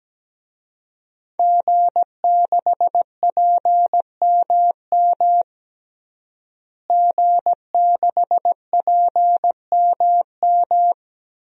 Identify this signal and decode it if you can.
Morse code, heard as a keyed tone, about 17 wpm, 700 Hz. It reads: G6PMM G6PMM
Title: DXCC-100 Call Signs - 5 Characters - Repeated 2x - 17wpm